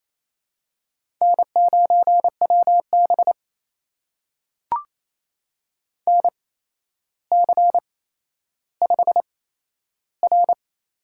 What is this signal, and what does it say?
Morse code, heard as a keyed tone, about 27 wpm, spaced as for 12 wpm, 700 Hz. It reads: N9WB NC5R
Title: USA Call Signs - 4 Characters - 8x Character Spacing - 28wpm